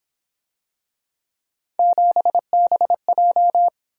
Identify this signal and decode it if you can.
Morse code, heard as a keyed tone, about 28 wpm, 700 Hz. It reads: 7BJ
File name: Speed-Racing - Three Letter-Number Combinations - 22wpm